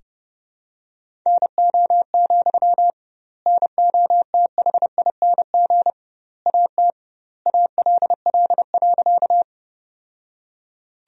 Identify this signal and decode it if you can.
Morse code, heard as a keyed tone, about 32 wpm, 700 Hz. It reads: NO, NOTHING AT ALL.
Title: 2x Word-Spacing - Sentences from Top 500 Words - 30wpm